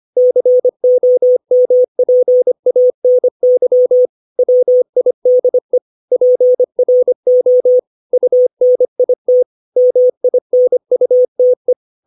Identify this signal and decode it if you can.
Morse code, heard as a keyed tone, about 25 wpm, 500 Hz. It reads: COMPANY WIDE PRO UNIT MINUTE
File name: 632 words / 632 Words 25WPM